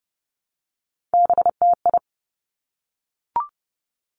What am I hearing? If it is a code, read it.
Morse code, heard as a keyed tone, about 30 wpm, 700 Hz. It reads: BTI
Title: Three Challenging Letter-Number Combinations 30wpm